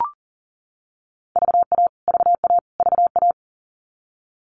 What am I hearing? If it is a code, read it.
Morse code, heard as a keyed tone, about 40 wpm, 700 Hz. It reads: VA VA VA
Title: US States Repeated 3x 40wpm